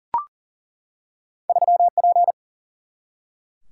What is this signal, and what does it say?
Morse code, heard as a keyed tone, about 40 wpm, 700 Hz. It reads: 3P
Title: Single-Letter Single-Number 40wpm